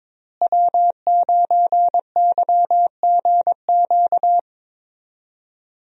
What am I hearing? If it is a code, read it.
Morse code, heard as a keyed tone, about 22 wpm, 700 Hz. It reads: W9YGQ
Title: USA Call Signs - 5 Characters - Repeated 3x - 22wpm